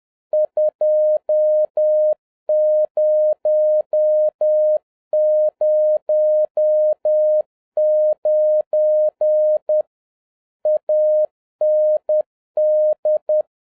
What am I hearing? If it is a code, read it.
Morse code, heard as a keyed tone, about 10 wpm, 610 Hz. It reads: 2009 AND